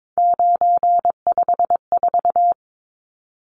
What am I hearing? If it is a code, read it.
Morse code, heard as a keyed tone, about 22 wpm, 700 Hz. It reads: 954
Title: Numbers - 3-Digit 22wpm